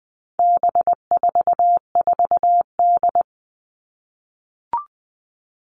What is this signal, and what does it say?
Morse code, heard as a keyed tone, about 20 wpm, 700 Hz. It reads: B44D
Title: All Cut Numbers - 4-Digit 20wpm